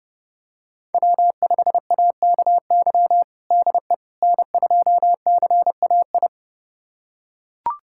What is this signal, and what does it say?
Morse code, heard as a keyed tone, about 30 wpm, 700 Hz. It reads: W5AKY DE N2CAI
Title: USA Callsign DE Callsign - 5 Characters - 30wpm